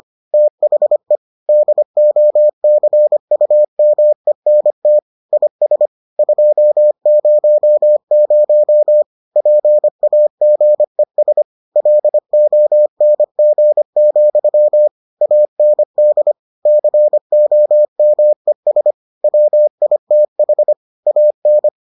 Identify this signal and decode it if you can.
Morse code, heard as a keyed tone, about 25 wpm, 600 Hz. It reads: THE DOCUMENT IS 200 PAGES LONG, AND COMES WITH AN